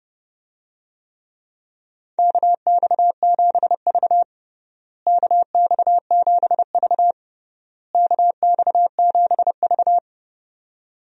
Morse code, heard as a keyed tone, about 30 wpm, 700 Hz. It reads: KX7V KX7V KX7V
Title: USA Call Signs - 4 Characters - Repeated 3x - 30wpm